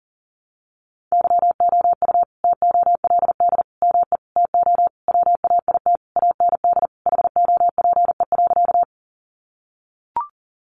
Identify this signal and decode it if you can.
Morse code, heard as a keyed tone, about 37 wpm, 700 Hz. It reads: YOU TOLD ME TO WAIT AND HOPE.
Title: Sentences from Top 1000 Words 40wpm